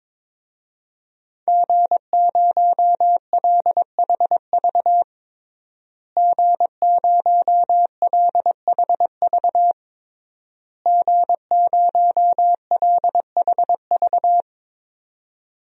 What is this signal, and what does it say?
Morse code, heard as a keyed tone, about 22 wpm, 700 Hz. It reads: G0LHV G0LHV G0LHV
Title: DXCC-100 Call Signs - 5 Characters - Repeated 3x - 22wpm